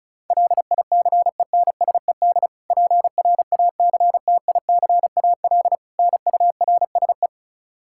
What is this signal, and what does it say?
Morse code, heard as a keyed tone, about 34 wpm, 700 Hz. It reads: LICENSED PRACTICAL NURSE
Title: Practice QSO - Exchange 3 - More Info - 35wpm